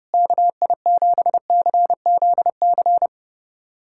Morse code, heard as a keyed tone, about 31 wpm, 700 Hz. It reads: KI7CZC
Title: USA Callsign DE Callsign - 6 Characters - 30wpm